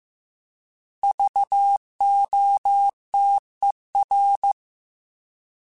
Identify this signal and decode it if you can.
Morse code, heard as a keyed tone, about 15 wpm, 800 Hz. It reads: VOTER
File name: Alphabetical / Alphabetical mp3s / Vs @13